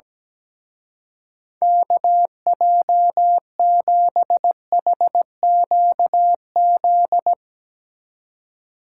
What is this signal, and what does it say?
Morse code, heard as a keyed tone, about 17 wpm, 700 Hz. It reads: KJ7HQZ
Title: Speed Racing - USA 2x3 Call Signs 17wpm